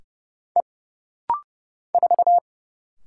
Morse code, heard as a keyed tone, about 31 wpm, 700 Hz. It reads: E 4
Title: Single Letter-Number Alt - Rapid-Fire 30wpm